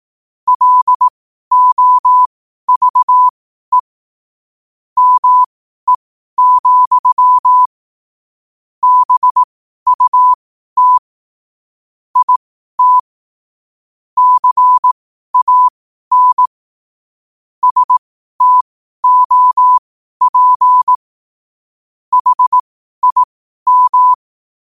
Morse code, heard as a keyed tone, about 18 wpm, spaced as for 12 wpm, 1000 Hz. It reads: LOVE ME, BUT IT CAN STOP HIM
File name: qotdfiles / 12wpm